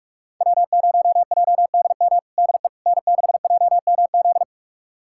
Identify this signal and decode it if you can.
Morse code, heard as a keyed tone, about 44 wpm, 700 Hz. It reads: W0JDM DE N6JMZ
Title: USA Callsign DE Callsign 45wpm